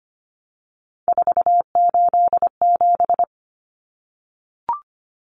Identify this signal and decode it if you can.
Morse code, heard as a keyed tone, about 25 wpm, 700 Hz. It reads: 487
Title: Common Cut Numbers - 3-Digit 25wpm